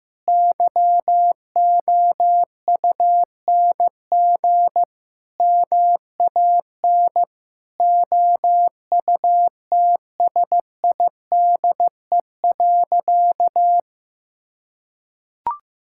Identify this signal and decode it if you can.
Morse code, heard as a keyed tone, about 15 wpm, 700 Hz. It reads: YOUNG MAN OUTSIDE.
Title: Sentences from Top 1000 Words - Review Part 6 - 15wpm